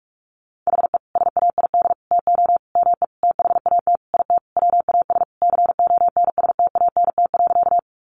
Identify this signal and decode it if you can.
Morse code, heard as a keyed tone, about 44 wpm, 700 Hz. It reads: HE SAID TO ME THAT IT WAS CONSTANT.
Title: Sentences from Top 1500 Words 45wpm